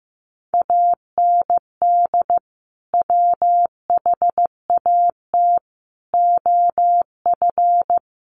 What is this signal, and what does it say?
Morse code, heard as a keyed tone, about 15 wpm, 700 Hz. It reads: AND WHAT OF
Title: Sentences from Top 1500 Words 15wpm